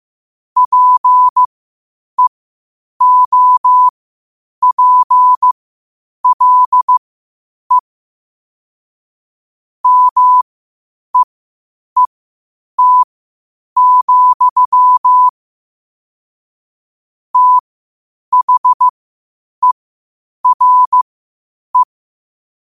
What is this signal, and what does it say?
Morse code, heard as a keyed tone, about 15 wpm, spaced as for 8 wpm, 1000 Hz. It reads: PEOPLE MEET, THERE
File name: qotdfiles / 8wpm